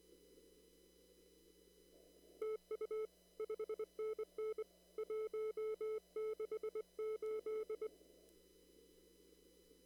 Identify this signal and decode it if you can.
Morse code, heard as a keyed tone, about 23 wpm, 430 Hz. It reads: TU 5NN 168
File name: CW / kt5j wpx cw-op-k5pi